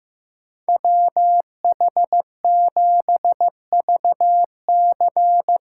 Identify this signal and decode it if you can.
Morse code, heard as a keyed tone, about 16 wpm, 700 Hz. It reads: WH7VC
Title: Speed Racing - USA Call Signs 15wpm